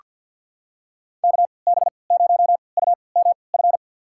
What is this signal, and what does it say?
Morse code, heard as a keyed tone, about 49 wpm, spaced as for 28 wpm, 700 Hz. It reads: KB0UMF
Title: USA Call Signs - 6 Characters - 3x Character Spacing - 50wpm